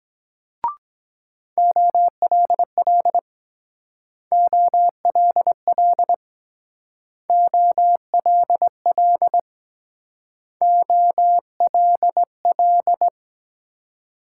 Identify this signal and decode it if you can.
Morse code, heard as a keyed tone, about 21 wpm, 700 Hz. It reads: OLL OLL OLL OLL
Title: Speed-Racing Multi-letter Phonemes - 17wpm